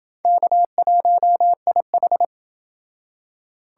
Morse code, heard as a keyed tone, about 29 wpm, 700 Hz. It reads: K1IH